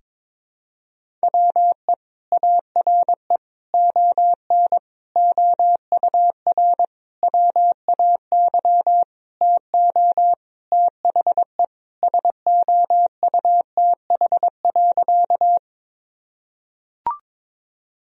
Morse code, heard as a keyed tone, about 22 wpm, 700 Hz. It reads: WE ARE ON OUR WAY TO THE SOUTH.
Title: Sentences from Top 1000 Words - Review Part 2 - 22wpm